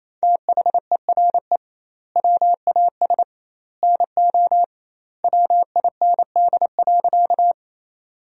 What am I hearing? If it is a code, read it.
Morse code, heard as a keyed tone, about 27 wpm, 700 Hz. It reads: THERE WAS NO WIND.
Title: 2x Word-Spacing - Sentences from Top 400 Words - 28wpm